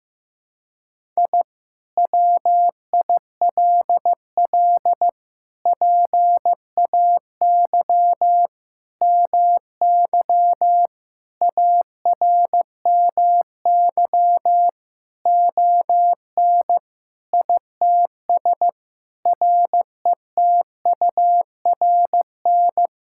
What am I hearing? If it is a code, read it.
Morse code, heard as a keyed tone, about 15 wpm, 700 Hz. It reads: I WILL PAY MY ARMY ON ITS RETURN
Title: Sentences from Top 1000 Words - Review Part 5 - 15wpm